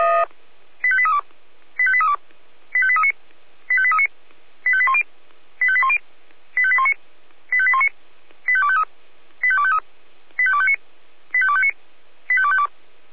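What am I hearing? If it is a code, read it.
Five-tone selective call (ZVEI), heard as quick runs of five tones, tones of about 70 ms. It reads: 85832 85832 85839 85839 85810 85810 85819 85819 85253 85253 85259 85259 85252
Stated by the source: ZVEI five-tone selective call received on SDR.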